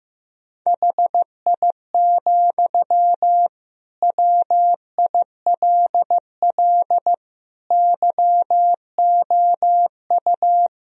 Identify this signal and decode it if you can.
Morse code, heard as a keyed tone, about 15 wpm, 700 Hz. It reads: HI, WILL YOU
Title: Sentences from Top 1000 Words - Review Part 3 - 15wpm